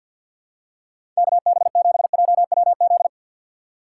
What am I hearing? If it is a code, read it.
Morse code, heard as a keyed tone, about 52 wpm, 700 Hz. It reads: KB7JWZ
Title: USA Call Signs - 6 Characters - 6x Character Spacing - 50wpm